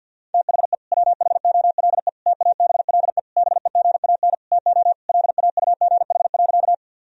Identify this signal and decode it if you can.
Morse code, heard as a keyed tone, about 51 wpm, 700 Hz. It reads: THE WHOLE TABLE BEGAN TO LAUGH.
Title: Sentences from Top 1000 Words - Review Part 2 - 50wpm